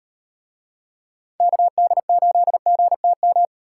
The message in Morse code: KD8GTM

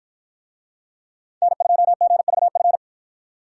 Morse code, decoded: N2GVF

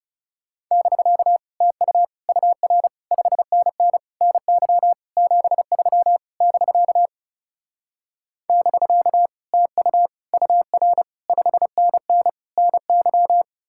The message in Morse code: <BK> TU UR 5NN NY 73 <BK> <BK> TU UR 5NN NY

